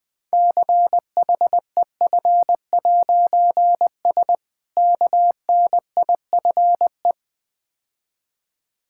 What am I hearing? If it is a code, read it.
Morse code, heard as a keyed tone, about 20 wpm, 700 Hz. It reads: CHEF'S KNIFE
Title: Difficult Words - Speed-Racing with Hints - 20wpm